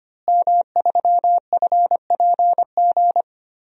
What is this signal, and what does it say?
Morse code, heard as a keyed tone, about 25 wpm, 700 Hz. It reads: M3FPG
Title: DXCC-100 Call Signs - 5 Characters - Repeated 3x - 25wpm